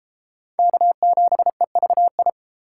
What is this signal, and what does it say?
Morse code, heard as a keyed tone, about 33 wpm, 700 Hz. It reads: K7EVI